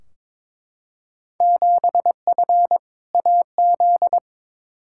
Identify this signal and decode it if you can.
Morse code, heard as a keyed tone, about 22 wpm, 700 Hz. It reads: 7F AZ
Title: Field Day Exchanges Encore - 22wpm